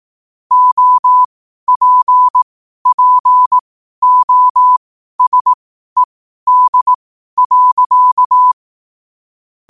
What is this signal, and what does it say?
Morse code, heard as a keyed tone, about 18 wpm, spaced as for 12 wpm, 1000 Hz. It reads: OPPOSED.
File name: qotdfiles / 12wpm